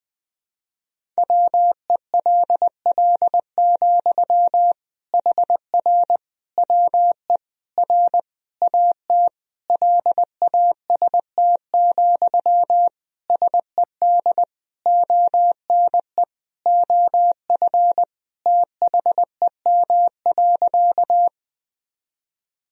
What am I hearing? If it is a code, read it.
Morse code, heard as a keyed tone, about 20 wpm, 700 Hz. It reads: WELL, HR WE R AT LAST, SED ONE OF THEM.
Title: Sentences from Top 300 Words and Abbreviations 20wpm